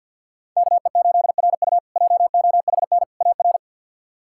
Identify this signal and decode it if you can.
Morse code, heard as a keyed tone, about 52 wpm, 700 Hz. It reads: KE8RU JOHN AR
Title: K1USN SST QSO Practice - 50wpm